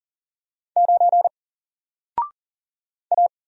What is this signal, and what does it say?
Morse code, heard as a keyed tone, about 40 wpm, 700 Hz. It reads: 9 A
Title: Single Letter-Number Alt 40wpm